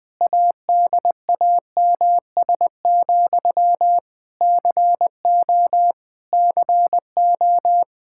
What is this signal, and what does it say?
Morse code, heard as a keyed tone, about 20 wpm, 700 Hz. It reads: ADAMS, CO CO
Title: K1USN SST QSO Practice 20wpm